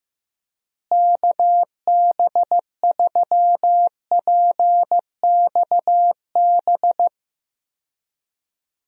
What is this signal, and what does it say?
Morse code, heard as a keyed tone, about 15 wpm, 700 Hz. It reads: KB3PXB
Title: USA Call Signs - 6 Characters - Repeated 2x - 15wpm